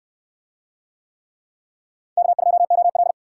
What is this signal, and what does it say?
Morse code, heard as a keyed tone, about 60 wpm, 700 Hz. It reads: K2OL